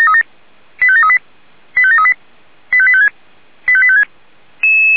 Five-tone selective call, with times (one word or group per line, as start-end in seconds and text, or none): 0.8-1.2 85738
1.8-2.1 85738
2.7-3.1 85856
3.7-4.0 85856